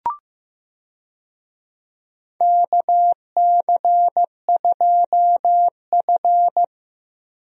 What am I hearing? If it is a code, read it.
Morse code, heard as a keyed tone, about 15 wpm, 700 Hz. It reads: KC2F
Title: USA Call Signs - 4 Characters - Repeated 2x - 15wpm